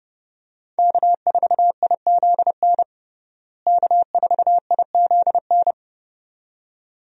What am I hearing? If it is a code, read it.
Morse code, heard as a keyed tone, about 30 wpm, 700 Hz. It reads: K4IZN K4IZN
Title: USA Call Signs - 5 Characters - Repeated 2x - 30wpm